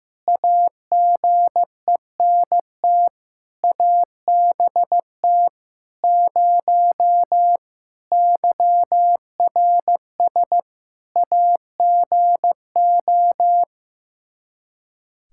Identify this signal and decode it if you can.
Morse code, heard as a keyed tone, about 15 wpm, 700 Hz. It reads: AGENT ABT 0 YRS AGO
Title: Practice QSO - Exchange 3 - More Info - 15wpm